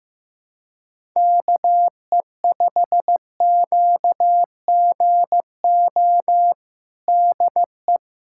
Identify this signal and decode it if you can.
Morse code, heard as a keyed tone, about 15 wpm, 700 Hz. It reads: KE5QGO DE
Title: USA Callsign DE Callsign - 6 Characters - 15wpm